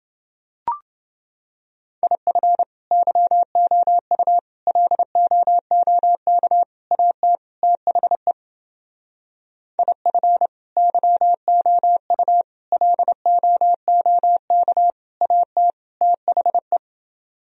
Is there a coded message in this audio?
Yes: Morse code from 2.0 to 16.8 s, heard as a keyed tone, about 29 wpm, 700 Hz. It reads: IF YOU LOOK AT THE IF YOU LOOK AT THE